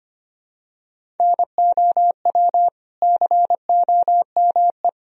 Morse code, heard as a keyed tone, about 25 wpm, 700 Hz. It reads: NOW COME
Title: Sentences from Top 200 Words and Abbreviations 25wpm